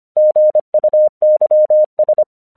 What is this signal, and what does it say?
Morse code, heard as a keyed tone, about 25 wpm, 610 Hz. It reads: GUYS
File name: 1 English 25wpm / AQC32